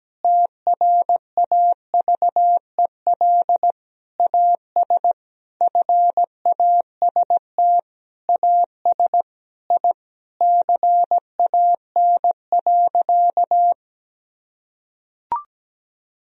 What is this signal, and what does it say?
Morse code, heard as a keyed tone, about 17 wpm, 700 Hz. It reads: TRAVEL AS FAST AS I CAN.
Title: Sentences from Top 1000 Words - Review Part 4 - 17wpm